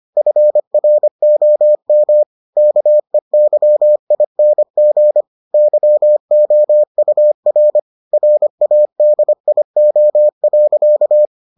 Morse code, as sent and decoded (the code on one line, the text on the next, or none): ..-. .-. --- -- / -.- . -.-- .. -. --. / -.-- --- ..- .-. / .-. .- -.. .. --- .-.-.-
FROM KEYING YOUR RADIO.